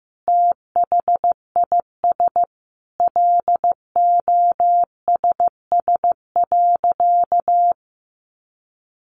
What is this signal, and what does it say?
Morse code, heard as a keyed tone, about 14 wpm, 700 Hz. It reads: THIS LOSS.
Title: Sentences from Top 1400 Words 15wpm